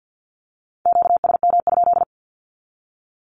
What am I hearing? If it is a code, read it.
Morse code, heard as a keyed tone, about 48 wpm, 700 Hz. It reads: QSM?
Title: Q-Codes 50wpm